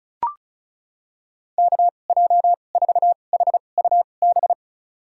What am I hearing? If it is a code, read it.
Morse code, heard as a keyed tone, about 34 wpm, spaced as for 25 wpm, 700 Hz. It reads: KJ4HUB
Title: USA Call Signs - 6 Characters - 2x Character Spacing - 35wpm